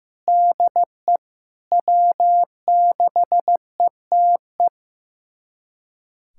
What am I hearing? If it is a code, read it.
Morse code, heard as a keyed tone, about 16 wpm, 700 Hz. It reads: DE W6ETE
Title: USA Callsign DE Callsign 15wpm